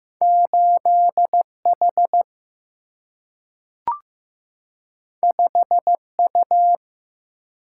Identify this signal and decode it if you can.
Morse code, heard as a keyed tone, about 15 wpm, 700 Hz. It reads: 8 H 5 U
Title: Single-Letter Single-Number 15wpm